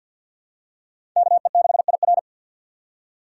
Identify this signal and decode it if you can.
Morse code, heard as a keyed tone, about 47 wpm, 700 Hz. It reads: KE6IR
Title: USA Call Signs - 5 Characters - 6x Character Spacing - 50wpm